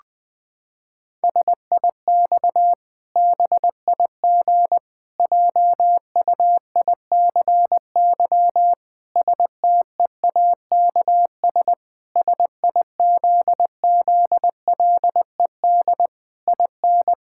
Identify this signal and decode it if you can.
Morse code, heard as a keyed tone, about 20 wpm, 700 Hz. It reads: SIX BIG JUICY STEAKS SIZZLED IN
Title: Pangram Challenge v2 - 20wpm